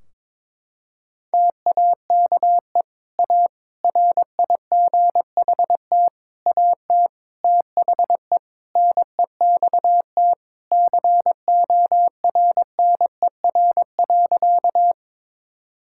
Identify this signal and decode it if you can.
Morse code, heard as a keyed tone, about 22 wpm, 700 Hz. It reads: TAKE A RIGHT AT THE NEXT CORNER.
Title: Sentences from Top 1600 Words 22wpm